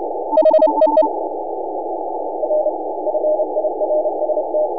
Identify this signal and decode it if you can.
Morse code, heard as a keyed tone, about 29 wpm, 610 Hz, with noise. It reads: HEE R FIRST
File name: n6tr / N6AA Skimmed